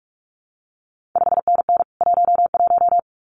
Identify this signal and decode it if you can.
Morse code, heard as a keyed tone, about 46 wpm, 700 Hz. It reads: 5NN 11
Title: CQ WW DX Contest Exchange - 45wpm